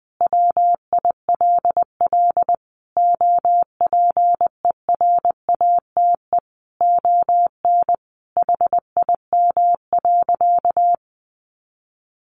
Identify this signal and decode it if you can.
Morse code, heard as a keyed tone, about 20 wpm, 700 Hz. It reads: WILL OPERATE ON HIM.
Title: Sentences from Top 1500 Words 20wpm